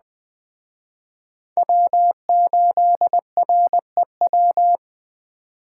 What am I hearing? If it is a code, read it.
Morse code, heard as a keyed tone, about 20 wpm, 700 Hz. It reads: W8REW
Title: Speed Racing - USA 1x3 Call Signs 17wpm